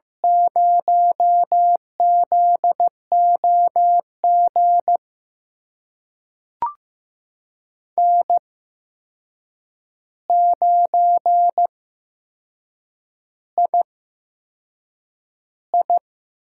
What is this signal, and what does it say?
Morse code, heard as a keyed tone, about 15 wpm, spaced as for 6 wpm, 700 Hz. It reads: 0ZOG N9II
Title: USA Call Signs - 5 Characters - 8x Character Spacing - 15wpm